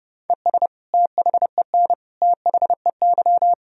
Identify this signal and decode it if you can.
Morse code, heard as a keyed tone, about 30 wpm, 700 Hz. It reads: ES THEN THEY
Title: Sentences from Top 300 Words and Abbreviations 30wpm